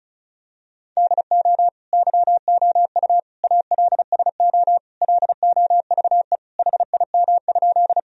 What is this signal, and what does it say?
Morse code, heard as a keyed tone, about 36 wpm, 700 Hz. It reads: DO YOU ALSO LOVE HIM?
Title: Sentences from Top 1000 Words - Review Part 1 - 35wpm